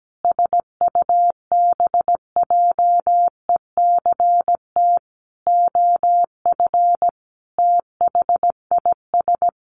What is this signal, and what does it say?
Morse code, heard as a keyed tone, about 17 wpm, 700 Hz. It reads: SUBJECT OF THIS
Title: Sentences from Top 1100 Words 17wpm